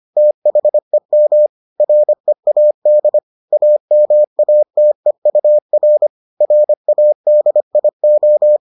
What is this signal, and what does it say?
Morse code, heard as a keyed tone, about 25 wpm, 600 Hz. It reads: THEM READ AMATEUR RADIO